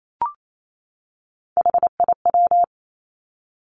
Morse code, heard as a keyed tone, about 28 wpm, 700 Hz. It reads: HIW
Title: Three Challenging Letter-Number Combinations 28wpm